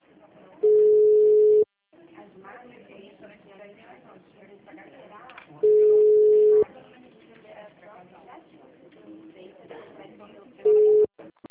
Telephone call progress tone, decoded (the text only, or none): ringback tone